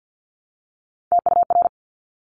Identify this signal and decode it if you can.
Morse code, heard as a keyed tone, about 50 wpm, 700 Hz. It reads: TUR
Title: Top Three Letter Combinations in Words 50wpm